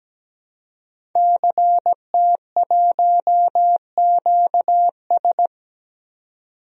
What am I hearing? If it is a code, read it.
Morse code, heard as a keyed tone, about 17 wpm, 700 Hz. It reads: CT1QS
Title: DXCC-100 Call Signs - 5 Characters - 17wpm